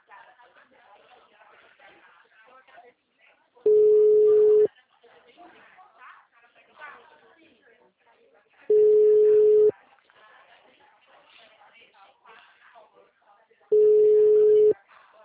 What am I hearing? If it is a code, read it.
Ringback tone (Europe, 425 Hz), with speech.
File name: out- -2027- - - .80291